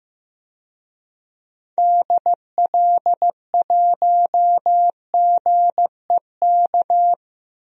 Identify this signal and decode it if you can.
Morse code, heard as a keyed tone, about 15 wpm, 700 Hz. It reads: DL1GEK
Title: DXCC-100 Call Signs - 6 Characters - Repeated 2x - 15wpm